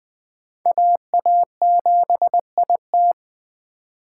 Morse code, heard as a keyed tone, about 20 wpm, 700 Hz. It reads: AA7IT
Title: Speed Racing - USA 2x2 Call Signs 17wpm